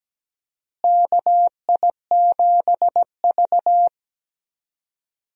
Morse code, heard as a keyed tone, about 17 wpm, 700 Hz. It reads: KI7V
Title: USA Call Signs - 4 Characters - Repeated 2x - 17wpm